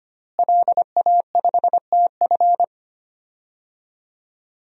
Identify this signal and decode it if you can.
Morse code, heard as a keyed tone, about 25 wpm, 700 Hz. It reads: LA5TF
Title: DXCC-100 Call Signs - 5 Characters - Repeated 2x - 25wpm